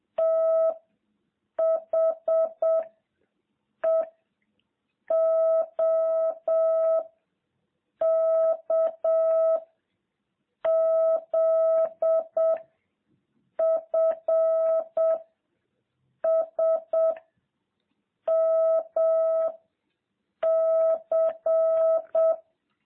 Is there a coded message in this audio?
Yes: Morse code from 0.2 to 22.3 s, heard as a keyed tone, about 7 wpm, 650 Hz. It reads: T H E O K Z F S M C